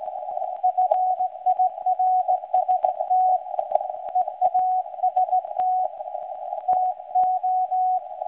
Morse code, heard as a keyed tone, about 17 wpm, 740 Hz. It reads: FIRST EAST TO
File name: WebSDR records / SAQ recording start 17.2kHz